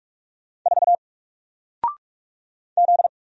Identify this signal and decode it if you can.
Morse code, heard as a keyed tone, about 45 wpm, 700 Hz. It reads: 4 Z
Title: Single Letter-Number Alt 45wpm